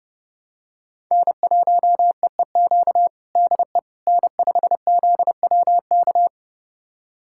Morse code, heard as a keyed tone, about 30 wpm, 700 Hz. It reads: N1EEQ DE N5ZWK
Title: USA Callsign DE Callsign - 5 Characters - 30wpm